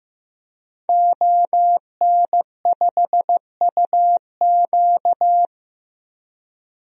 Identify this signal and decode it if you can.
Morse code, heard as a keyed tone, about 15 wpm, 700 Hz. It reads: ON5UQ
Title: DXCC-100 Call Signs - 5 Characters - 15wpm